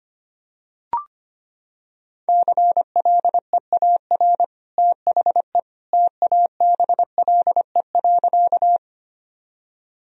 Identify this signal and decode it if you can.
Morse code, heard as a keyed tone, about 25 wpm, 700 Hz. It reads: CLEAR THE TABLE.
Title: Sentences from Top 1000 Words - Review Part 2 - 25wpm